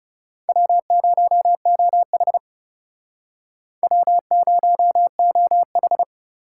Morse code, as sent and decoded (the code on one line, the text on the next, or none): .-- ----- --- .... / .-- ----- --- ....
W0OH W0OH